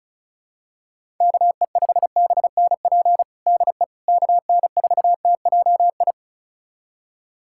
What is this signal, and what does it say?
Morse code, heard as a keyed tone, about 36 wpm, 700 Hz. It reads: KE5BNP DE KN4TJI
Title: USA Callsign DE Callsign 35wpm